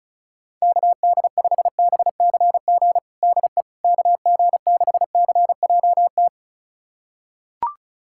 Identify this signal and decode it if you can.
Morse code, heard as a keyed tone, about 36 wpm, 700 Hz. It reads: KD5BCG DE KG6CJT
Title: USA Callsign DE Callsign 35wpm